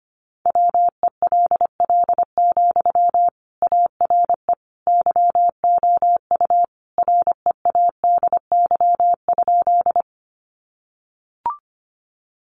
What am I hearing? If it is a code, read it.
Morse code, heard as a keyed tone, about 25 wpm, 700 Hz. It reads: WELL, ARE YOU READY?
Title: Sentences from Top 400 Words 25wpm